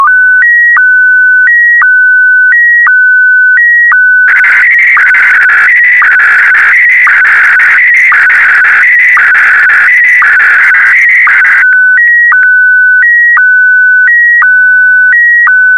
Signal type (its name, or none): SSTV (slow-scan television)